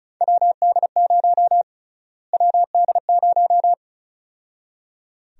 Morse code, heard as a keyed tone, about 34 wpm, 700 Hz. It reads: WD0 WD0
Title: USA Call Sign Prefixes - Repeated 2x - 35wpm